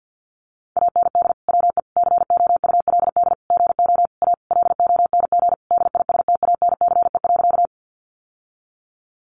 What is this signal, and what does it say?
Morse code, heard as a keyed tone, about 50 wpm, 700 Hz. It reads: AND WE COULD GO A LONG DISTANCE.
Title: Sentences from Top 1200 Words 50wpm